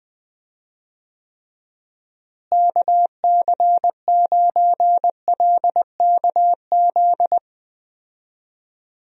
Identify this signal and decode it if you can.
Morse code, heard as a keyed tone, about 20 wpm, 700 Hz. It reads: KC9LKZ
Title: USA Call Signs - 6 Characters - Repeated 2x - 20wpm